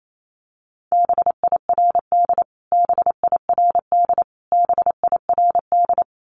Morse code, heard as a keyed tone, about 28 wpm, 700 Hz. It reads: BIRD BIRD BIRD
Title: Top 500 Words Repeated 28wpm